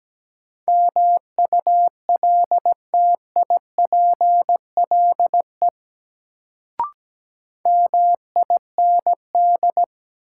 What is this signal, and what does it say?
Morse code, heard as a keyed tone, about 17 wpm, 700 Hz. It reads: MULTIPLE MIND
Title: Sets of 5 Words - New Word Drill - 17wpm